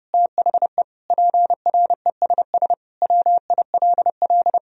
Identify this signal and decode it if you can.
Morse code, heard as a keyed tone, about 30 wpm, 700 Hz. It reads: THE PRESS WILL
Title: Sentences from Top 1000 Words - Review Part 1 - 30wpm